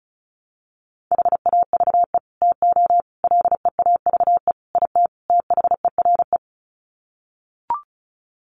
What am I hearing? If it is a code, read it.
Morse code, heard as a keyed tone, about 34 wpm, 700 Hz. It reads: HAVE TO LEAVE IT THERE
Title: Sets of 5 Words Encore2 35wpm